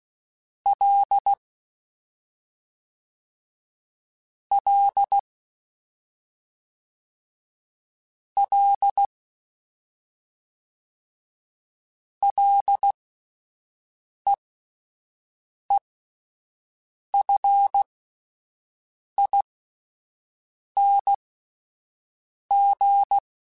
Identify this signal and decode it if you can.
Morse code, heard as a keyed tone, about 16 wpm, spaced as for 5 wpm, 800 Hz. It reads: L L L LEEFING
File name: Ls / Single L m3s / Single L Prefixe 3x2 Twice @5